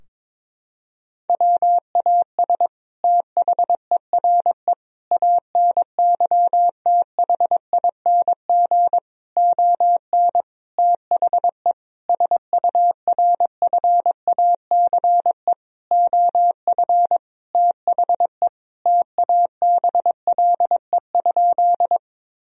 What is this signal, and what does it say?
Morse code, heard as a keyed tone, about 22 wpm, 700 Hz. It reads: WAS THERE ANYTHING ON THE SURFACE OF THE TABLE?
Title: Sentences from Top 1000 Words - Review Part 3 - 22wpm